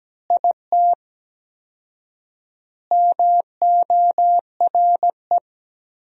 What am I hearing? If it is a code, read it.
Morse code, heard as a keyed tone, about 17 wpm, 700 Hz. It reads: IT MORE
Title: 4x Word-Spacing - Sentences from Top 400 Words - 17wpm